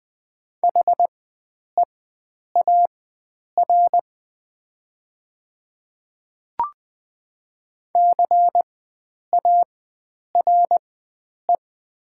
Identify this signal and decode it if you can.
Morse code, heard as a keyed tone, about 20 wpm, spaced as for 9 wpm, 690 Hz. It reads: HEAR CARE
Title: 4x Character Spacing - Top 500 Words - 20wpm